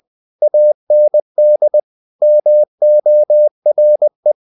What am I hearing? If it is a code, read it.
Morse code, heard as a keyed tone, about 20 wpm, 600 Hz. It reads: AND MORE
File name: the old man and the sea part 2 20wpm